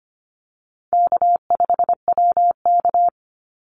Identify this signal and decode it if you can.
Morse code, heard as a keyed tone, about 25 wpm, 700 Hz. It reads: K5WK
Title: Easy Call Signs - 25wpm